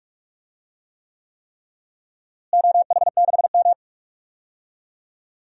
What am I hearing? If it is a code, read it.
Morse code, heard as a keyed tone, about 44 wpm, 700 Hz. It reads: OH6M
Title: DXCC-100 Call Signs - 4 Characters - Repeated 3x - 45wpm